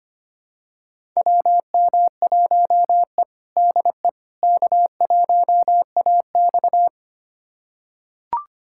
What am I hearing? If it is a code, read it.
Morse code, heard as a keyed tone, about 25 wpm, 700 Hz. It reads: WM1E DE K1AX
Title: USA Callsign DE Callsign - 4 Characters - 25wpm